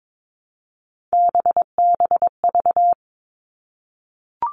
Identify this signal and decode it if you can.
Morse code, heard as a keyed tone, about 22 wpm, 700 Hz. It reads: BBV
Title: All Cut Numbers - 3-Digit 22wpm